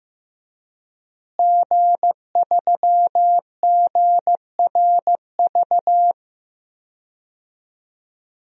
Morse code, heard as a keyed tone, about 15 wpm, 700 Hz. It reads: G3GRV
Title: DXCC-100 Call Signs - 5 Characters - Repeated 3x - 15wpm